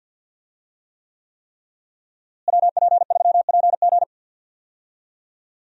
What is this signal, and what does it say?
Morse code, heard as a keyed tone, about 52 wpm, 700 Hz. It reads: WP3PG
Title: USA Call Signs - 5 Characters - Repeated 3x - 50wpm